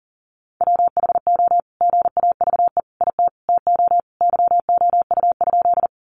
Morse code, heard as a keyed tone, about 40 wpm, 700 Hz. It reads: WHO GAVE IT TO YOU?
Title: Sentences from Top 500 Words 40wpm